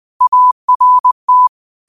Morse code, heard as a keyed tone, about 20 wpm, 1000 Hz. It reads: ART